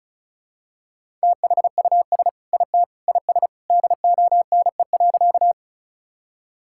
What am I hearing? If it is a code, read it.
Morse code, heard as a keyed tone, about 36 wpm, 700 Hz. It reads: THUS IT IS DONE.
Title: Sentences from Top 1000 Words - Review Part 5 - 35wpm